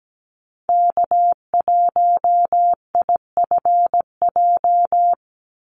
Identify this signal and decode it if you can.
Morse code, heard as a keyed tone, about 17 wpm, 700 Hz. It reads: K1IFJ
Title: Call Signs Repeated 3x - 17wpm